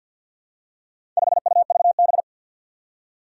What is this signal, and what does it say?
Morse code, heard as a keyed tone, about 54 wpm, 700 Hz. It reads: 5UVB
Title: Four Challenging Letter-Number Combinations 50wpm